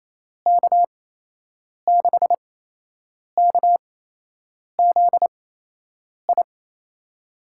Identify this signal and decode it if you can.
Morse code, heard as a keyed tone, about 28 wpm, spaced as for 8 wpm, 700 Hz. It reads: K6KZI
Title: USA Call Signs - 6 Characters - 8x Character Spacing - 28wpm